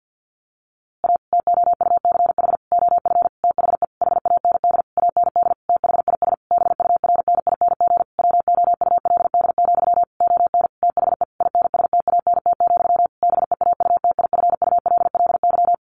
Morse code, heard as a keyed tone, about 50 wpm, 700 Hz. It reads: A TOUCH OF THE HAND AND THIS BURNING WOULD, ON THE INSTANT, BEAUTIFULLY
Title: A Sound Of Thunder 50wpm